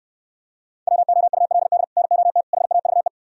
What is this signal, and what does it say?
Morse code, heard as a keyed tone, about 67 wpm, 700 Hz. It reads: WOULD NOT HAVE